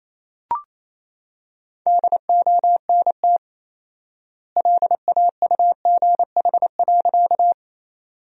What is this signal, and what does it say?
Morse code, heard as a keyed tone, about 28 wpm, 700 Hz. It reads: DONT LAUGH.
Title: 4x Word-Spacing - Sentences from Top 500 Words - 28wpm